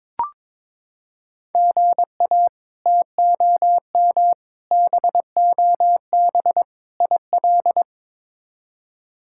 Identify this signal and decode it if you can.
Morse code, heard as a keyed tone, about 22 wpm, 700 Hz. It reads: GA TOM BOB IL
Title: K1USN SST QSO Practice - 22wpm